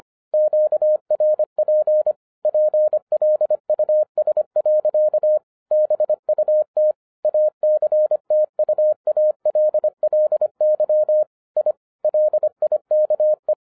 Morse code, heard as a keyed tone, about 25 wpm, 610 Hz. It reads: QRP PLUS. BUT ACTUALLY I LIKE
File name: 2011-07-mp3 / n24 QRP OTAKU 25wpm